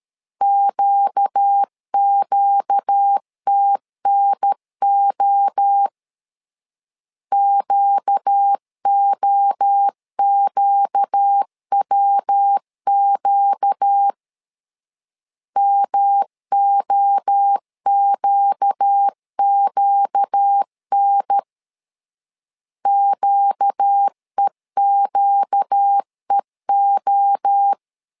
Morse code, heard as a keyed tone, about 13 wpm, 800 Hz. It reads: QQTNO QOQWQ MOQQN QEQEO
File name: LEZIONI / 4 Q 12wpm 1833